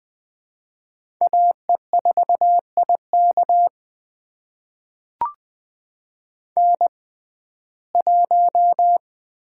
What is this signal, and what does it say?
Morse code, heard as a keyed tone, about 20 wpm, spaced as for 15 wpm, 700 Hz. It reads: AE4IK N1
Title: USA Call Signs - 5 Characters - 6x Character Spacing - 20wpm